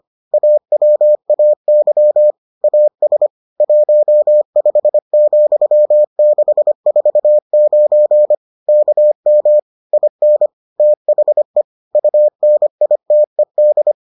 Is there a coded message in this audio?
Yes: Morse code from 0.3 to 13.9 s, heard as a keyed tone, about 25 wpm, 600 Hz. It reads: AWAY AS 15,649 KM IN THE UNITED